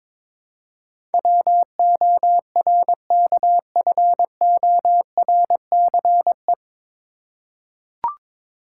Morse code, heard as a keyed tone, about 22 wpm, 700 Hz. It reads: WORKFORCE
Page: Morse Code Ninja practice 22wpm